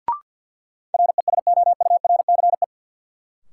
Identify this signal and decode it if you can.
Morse code, heard as a keyed tone, about 52 wpm, 700 Hz. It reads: RESOURCE
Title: Sentences from Top 1300 Words - New Word Drill Alt - 50wpm